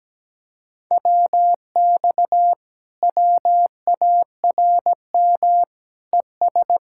Morse code, heard as a keyed tone, about 17 wpm, 700 Hz. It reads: WX WARM ES
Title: Speed-Racing Common QSO Phrases Encore - 17wpm